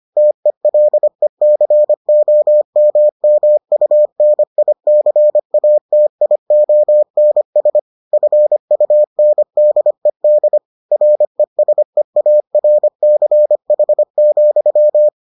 Morse code, heard as a keyed tone, about 25 wpm, 600 Hz. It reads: TELECOMMUNICATIONS FUNDED RESEARCH,